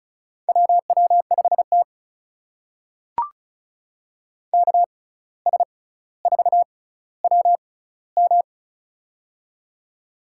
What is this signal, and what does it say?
Morse code, heard as a keyed tone, about 36 wpm, spaced as for 17 wpm, 700 Hz. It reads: WW5T KS4WM
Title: USA Call Signs - 6x Character Spacing - 35wpm